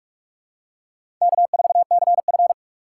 Morse code, heard as a keyed tone, about 46 wpm, 700 Hz. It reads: K4CF
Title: USA Call Signs - 4 Characters - 4x Character Spacing - 45wpm